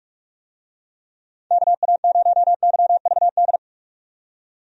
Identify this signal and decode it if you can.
Morse code, heard as a keyed tone, about 42 wpm, 700 Hz. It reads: KA0YVD
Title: USA Call Signs - 6 Characters - 45wpm